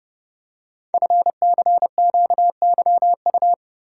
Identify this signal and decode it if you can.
Morse code, heard as a keyed tone, about 30 wpm, 690 Hz. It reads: FCQYU
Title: Code Groups - 5 Letters - 30wpm